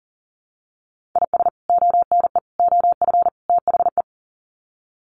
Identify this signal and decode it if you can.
Morse code, heard as a keyed tone, about 40 wpm, 700 Hz. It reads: IS ONE OF THE
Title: Sets of 4 Words 40wpm